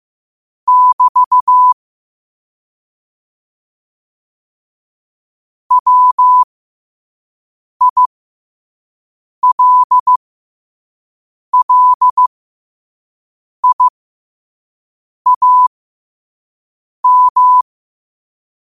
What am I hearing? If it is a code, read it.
Morse code, heard as a keyed tone, about 15 wpm, spaced as for 5 wpm, 1000 Hz. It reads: <BT> WILLIAM
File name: qotdfiles / 5wpm